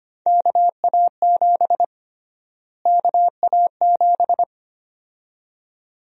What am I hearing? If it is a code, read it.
Morse code, heard as a keyed tone, about 25 wpm, 700 Hz. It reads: KA7 KA7
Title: USA Call Sign Prefixes - Repeated 2x - 25wpm